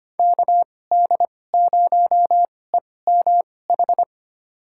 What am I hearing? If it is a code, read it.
Morse code, heard as a keyed tone, about 25 wpm, spaced as for 18 wpm, 700 Hz. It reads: KD0EMH
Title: USA Call Signs - 6 Characters - 2x Character Spacing - 25wpm